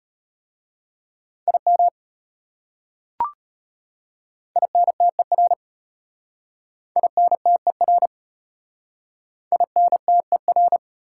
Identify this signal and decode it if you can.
Morse code, heard as a keyed tone, about 33 wpm, 700 Hz. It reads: IM INTER INTER INTER